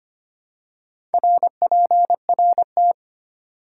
Morse code, heard as a keyed tone, about 25 wpm, 700 Hz. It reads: RPRT
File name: Top 90 QSO Elements - 4x Character Spacing - 25wpm